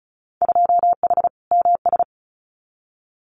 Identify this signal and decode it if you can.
Morse code, heard as a keyed tone, about 34 wpm, 700 Hz. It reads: 2H MS
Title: Winter Field Day Exchanges 35wpm